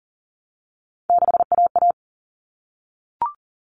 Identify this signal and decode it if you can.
Morse code, heard as a keyed tone, about 40 wpm, 700 Hz. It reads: 6AA
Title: All Cut Numbers - 3-Digit 40wpm